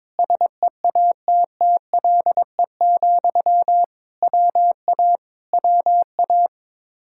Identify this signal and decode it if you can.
Morse code, heard as a keyed tone, about 22 wpm, 700 Hz. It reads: SEATTLE, WA WA
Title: Speed-Racing Common QSO Phrases - 22wpm